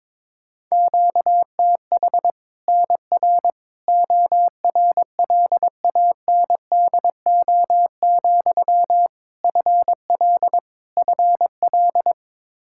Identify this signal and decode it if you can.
Morse code, heard as a keyed tone, about 22 wpm, 700 Hz. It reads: QTH NR ORLANDO, FL FL
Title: Practice QSO - Exchange 1 - Beginning - 22wpm